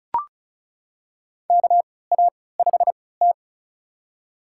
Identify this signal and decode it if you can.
Morse code, heard as a keyed tone, about 34 wpm, spaced as for 20 wpm, 700 Hz. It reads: KA5T
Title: USA Call Signs - 4 Characters - 3x Character Spacing - 35wpm